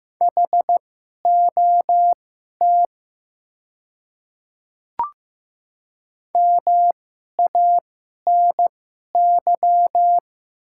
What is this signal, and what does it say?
Morse code, heard as a keyed tone, about 15 wpm, spaced as for 11 wpm, 700 Hz. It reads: HOT MANY
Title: 2x Character Spacing - Top 100 Words - 15wpm